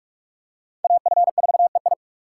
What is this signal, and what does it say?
Morse code, heard as a keyed tone, about 43 wpm, 700 Hz. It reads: AF4EI
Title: USA Call Signs - 5 Characters - 6x Character Spacing - 45wpm